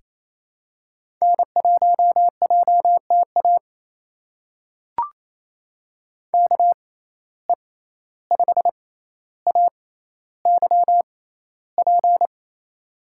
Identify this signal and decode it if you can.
Morse code, heard as a keyed tone, about 28 wpm, spaced as for 14 wpm, 700 Hz. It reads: N1JTA KE5AYP
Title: USA Call Signs - 6x Character Spacing - 28wpm